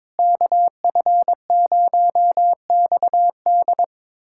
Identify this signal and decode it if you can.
Morse code, heard as a keyed tone, about 22 wpm, 700 Hz. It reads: KF0XD
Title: USA Callsign DE Callsign - 5 Characters - 22wpm